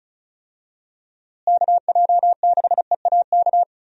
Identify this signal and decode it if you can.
Morse code, heard as a keyed tone, about 36 wpm, 700 Hz. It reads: KJ6EAK
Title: USA Call Signs - 6 Characters - Repeated 2x - 35wpm